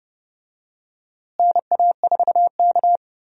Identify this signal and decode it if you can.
Morse code, heard as a keyed tone, about 30 wpm, 700 Hz. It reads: NA4K
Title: USA Call Signs - 4 Characters - Repeated 2x - 30wpm